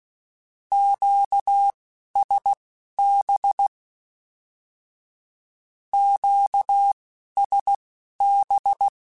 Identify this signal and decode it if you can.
Morse code, heard as a keyed tone, about 16 wpm, spaced as for 11 wpm, 800 Hz. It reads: QSB QSB
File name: Q Calls / Q Calls-mp3s / Q Calls Twice @10